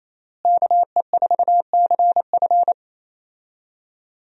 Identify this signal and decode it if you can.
Morse code, heard as a keyed tone, about 28 wpm, 700 Hz. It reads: KE4CF
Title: USA Call Signs - 5 Characters - Repeated 2x - 28wpm